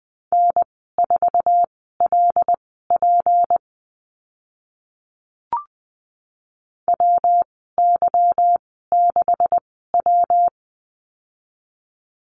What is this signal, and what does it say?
Morse code, heard as a keyed tone, about 20 wpm, spaced as for 15 wpm, 700 Hz. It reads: N4LP WY6W
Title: 161 Lesson 26g - Letter g - Call Signs 20wpm